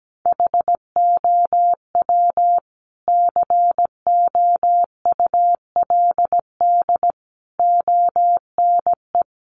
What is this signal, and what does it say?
Morse code, heard as a keyed tone, about 17 wpm, 700 Hz. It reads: HOW COULD ONE
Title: Sentences from Top 1100 Words 17wpm